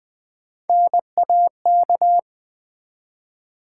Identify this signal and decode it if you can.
Morse code, heard as a keyed tone, about 20 wpm, 700 Hz. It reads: NAK
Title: Build A Word - 1 Letter At A Time - 5-Letter Words - 20wpm